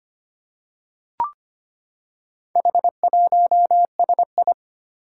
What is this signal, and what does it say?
Morse code, heard as a keyed tone, about 25 wpm, 700 Hz. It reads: H1SI
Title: Four Challenging Letter-Number Combinations 25wpm